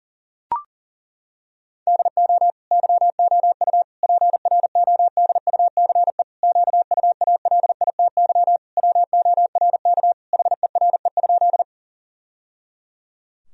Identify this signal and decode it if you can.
Morse code, heard as a keyed tone, about 40 wpm, 700 Hz. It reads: DO YOU PRODUCE QUALITY WORK HERE?
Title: Sentences from Top 1000 Words - Review Part 4 - 40wpm